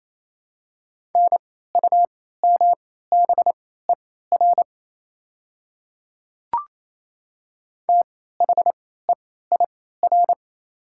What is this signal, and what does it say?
Morse code, heard as a keyed tone, about 28 wpm, spaced as for 16 wpm, 690 Hz. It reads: NUMBER THEIR
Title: 3x Character Spacing - Top 100 Words - 28wpm